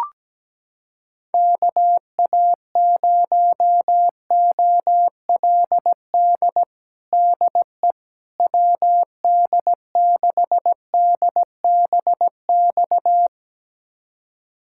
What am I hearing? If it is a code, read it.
Morse code, heard as a keyed tone, about 17 wpm, 700 Hz. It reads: KA0OLD DE WD6DBX
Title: USA Callsign DE Callsign - 6 Characters - 17wpm